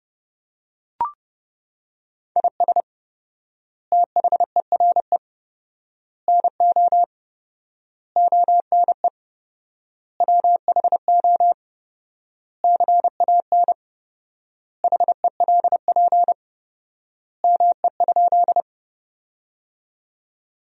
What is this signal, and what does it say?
Morse code, heard as a keyed tone, about 30 wpm, 700 Hz. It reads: IS THERE NO ONE WHO CAN HELP ME?
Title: 4x Word-Spacing - Sentences from Top 200 Words - 30wpm